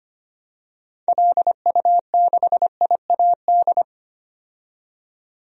Morse code, heard as a keyed tone, about 25 wpm, 700 Hz. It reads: LU6IAD
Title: DXCC-100 Call Signs - 6 Characters - 25wpm